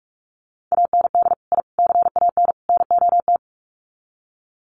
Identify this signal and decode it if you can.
Morse code, heard as a keyed tone, about 44 wpm, 700 Hz. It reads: AND I CAN NOT
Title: Sets of 4 Words 45wpm